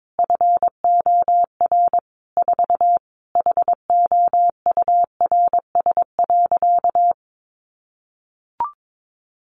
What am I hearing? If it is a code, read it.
Morse code, heard as a keyed tone, about 22 wpm, 700 Hz. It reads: FOR 4 HOURS.
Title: Additional Easy Phrases 22wpm